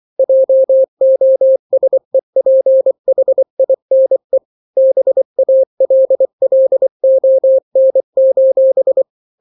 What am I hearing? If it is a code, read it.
Morse code, heard as a keyed tone, about 24 wpm, 530 Hz. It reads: JOSEPHINE BALLON:
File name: daily / 23wpm